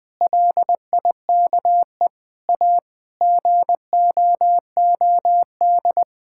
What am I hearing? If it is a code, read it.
Morse code, heard as a keyed tone, about 20 wpm, 700 Hz. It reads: LIKE A GOOD
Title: Sentences with 8-Word Phrases - 20wpm